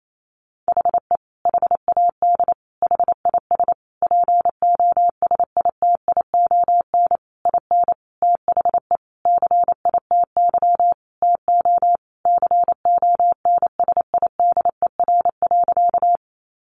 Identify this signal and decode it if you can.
Morse code, heard as a keyed tone, about 28 wpm, 700 Hz. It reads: HE HAD HIS POSITION IN THE CITY TO CONSIDER.
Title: Sentences from Top 900 Words 28wpm